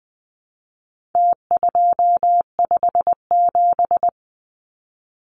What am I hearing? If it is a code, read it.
Morse code, heard as a keyed tone, about 20 wpm, 700 Hz. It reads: T257
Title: Numbers - Contest Style - 4-Digit 20wpm